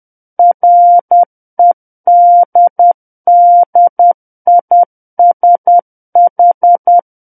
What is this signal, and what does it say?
Morse code, heard as a keyed tone, about 10 wpm, 700 Hz. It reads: REDDISH
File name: Book One / 01-Chapter 1 - 10 WPM